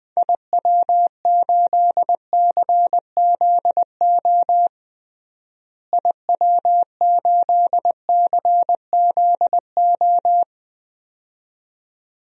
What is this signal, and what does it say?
Morse code, heard as a keyed tone, about 20 wpm, 700 Hz. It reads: IW8CZO IW8CZO
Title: DXCC-100 Call Signs - 6 Characters - Repeated 2x - 20wpm